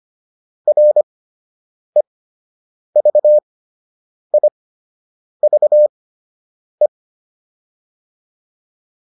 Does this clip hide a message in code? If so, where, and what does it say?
Yes: Morse code from 0.7 to 6.9 s, heard as a keyed tone, about 25 wpm, spaced as for 8 wpm, 600 Hz. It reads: REVIVE